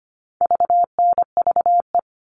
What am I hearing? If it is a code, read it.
Morse code, heard as a keyed tone, about 25 wpm, 700 Hz. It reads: VNVE
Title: All Cut Numbers - 4-Digit 25wpm